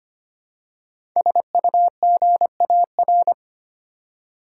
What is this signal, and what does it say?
Morse code, heard as a keyed tone, about 25 wpm, 690 Hz. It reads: SUGAR
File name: practice / 25wpm / Easy Words with Hints - 25wpm